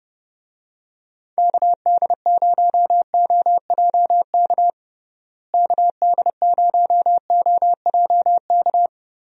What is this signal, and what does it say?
Morse code, heard as a keyed tone, about 32 wpm, 700 Hz. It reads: KD0OJK KD0OJK
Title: USA Call Signs - 6 Characters - Repeated 3x - 30wpm